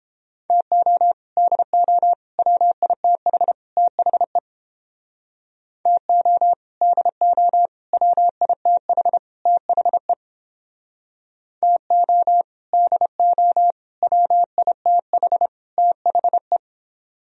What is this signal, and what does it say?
Morse code, heard as a keyed tone, about 30 wpm, 700 Hz. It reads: TO DO WITH THE TO DO WITH THE TO DO WITH THE